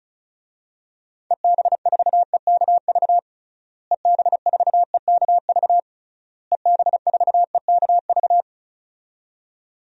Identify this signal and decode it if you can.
Morse code, heard as a keyed tone, about 34 wpm, 700 Hz. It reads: EB4EKV EB4EKV EB4EKV
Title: DXCC-100 Call Signs - 6 Characters - Repeated 3x - 35wpm